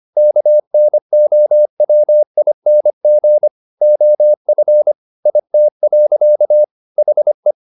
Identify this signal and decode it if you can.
Morse code, heard as a keyed tone, about 25 wpm, 600 Hz. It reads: KNOWING OF IT. HE